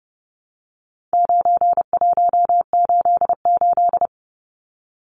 Morse code, heard as a keyed tone, about 30 wpm, 700 Hz. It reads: 9188
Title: Numbers - 4-Digit 30wpm